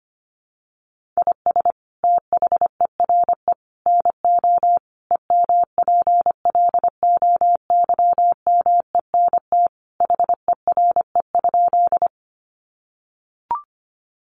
Morse code, heard as a keyed tone, about 25 wpm, 700 Hz. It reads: IS THERE NO EMPLOYMENT HERE?
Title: Sentences from Top 1500 Words 25wpm